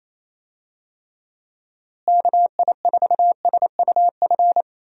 Morse code, heard as a keyed tone, about 28 wpm, 700 Hz. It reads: KI4SUF
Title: USA Call Signs - 6 Characters - Repeated 2x - 28wpm